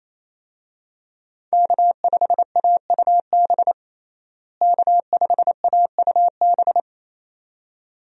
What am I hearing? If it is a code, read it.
Morse code, heard as a keyed tone, about 28 wpm, 700 Hz. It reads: K5AUB K5AUB
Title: USA Call Signs - 5 Characters - Repeated 2x - 28wpm